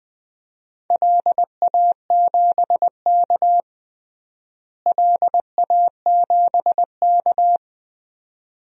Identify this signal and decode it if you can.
Morse code, heard as a keyed tone, about 20 wpm, 700 Hz. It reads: LA7K LA7K
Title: DXCC-100 Call Signs - 4 Characters - Repeated 2x - 20wpm